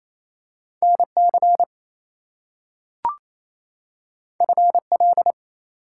Morse code, heard as a keyed tone, about 28 wpm, 700 Hz. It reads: NC FL
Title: US States and Canadian Provinces 28wpm